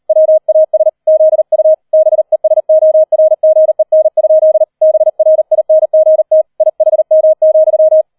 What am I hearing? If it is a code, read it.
Morse code, heard as a keyed tone, about 38 wpm, 630 Hz. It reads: WAS ZU BESORGEN? BRINGT IHM,
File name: 40wpm